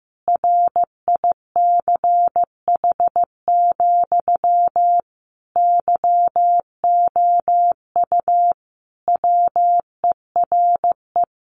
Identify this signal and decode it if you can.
Morse code, heard as a keyed tone, about 15 wpm, 700 Hz. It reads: RICH, YOU WERE
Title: Sentences from Top 1100 Words 15wpm